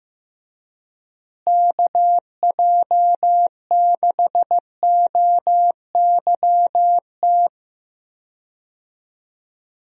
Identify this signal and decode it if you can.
Morse code, heard as a keyed tone, about 15 wpm, 700 Hz. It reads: KJ6OYT
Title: USA Call Signs - 6 Characters - Repeated 2x - 15wpm